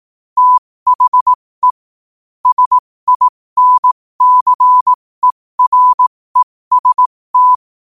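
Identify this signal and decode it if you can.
Morse code, heard as a keyed tone, about 18 wpm, spaced as for 15 wpm, 1000 Hz. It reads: THE SINCEREST
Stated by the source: Morse code quote of the day sent at 15 words per minute.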